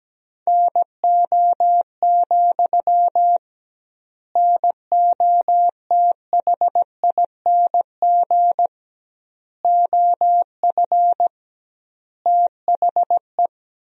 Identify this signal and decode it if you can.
Morse code, heard as a keyed tone, about 17 wpm, 700 Hz. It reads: NO, NOTHING OF THE
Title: 2x Word-Spacing - Sentences from Top 500 Words - 17wpm